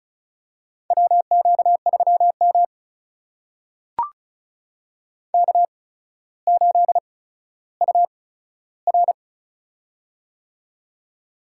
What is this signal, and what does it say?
Morse code, heard as a keyed tone, about 34 wpm, spaced as for 15 wpm, 700 Hz. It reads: WQ3M K8UR
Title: USA Call Signs - 4 Characters - 8x Character Spacing - 35wpm